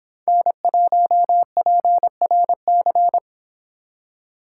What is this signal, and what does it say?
Morse code, heard as a keyed tone, about 26 wpm, 700 Hz. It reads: N1PRC